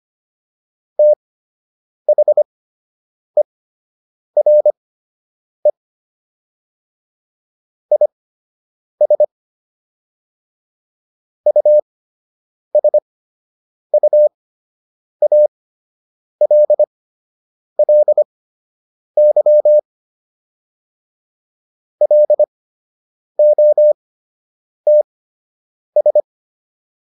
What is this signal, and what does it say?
Morse code, heard as a keyed tone, about 25 wpm, spaced as for 8 wpm, 600 Hz. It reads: THERE IS USUALLY LOTS